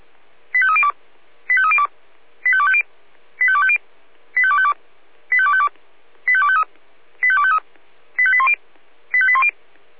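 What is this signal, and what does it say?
Five-tone selective call (ZVEI), heard as quick runs of five tones, tones of about 70 ms. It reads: 85302 85302 85250 85250 85252 85252 85253 85253 85810 85810
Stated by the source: Five-tone ZVEI selective call received by SDR.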